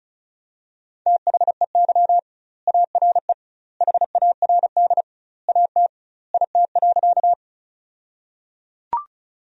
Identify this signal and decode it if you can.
Morse code, heard as a keyed tone, about 36 wpm, 700 Hz. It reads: THEY ARE HARD AT IT.
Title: 2x Word-Spacing - Sentences from Top 300 Words - 35wpm